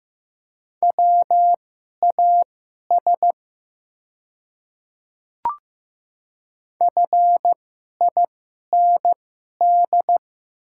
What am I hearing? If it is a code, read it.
Morse code, heard as a keyed tone, about 15 wpm, spaced as for 11 wpm, 700 Hz. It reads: WAS FIND
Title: 2x Character Spacing - Top 100 Words - 15wpm